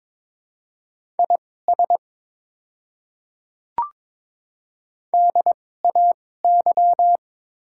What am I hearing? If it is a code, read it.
Morse code, heard as a keyed tone, about 22 wpm, spaced as for 16 wpm, 700 Hz. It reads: IS DAY
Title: 2x Character Spacing - Top 200 Words - 22wpm